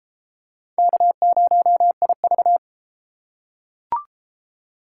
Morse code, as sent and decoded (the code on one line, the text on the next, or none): -.- ----- .. ...-
K0IV